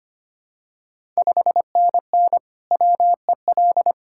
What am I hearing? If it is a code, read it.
Morse code, heard as a keyed tone, about 25 wpm, 700 Hz. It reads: 5NN WEL
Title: Ontario QSO Party Exchange 25wpm